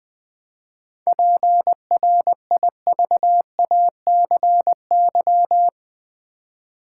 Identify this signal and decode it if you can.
Morse code, heard as a keyed tone, about 20 wpm, 690 Hz. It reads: PRIVACY
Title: 7 Letter Words - First 3 Letters Given - 20wpm